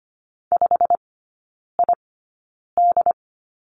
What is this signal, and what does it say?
Morse code, heard as a keyed tone, about 25 wpm, spaced as for 9 wpm, 690 Hz. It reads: 5ID
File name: Lesson 15e - Number 5 - Three Characters 25wpm